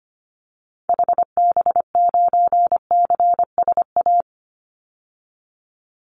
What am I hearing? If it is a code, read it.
Morse code, heard as a keyed tone, about 25 wpm, 700 Hz. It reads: HB9CSA
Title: Call Signs - 25wpm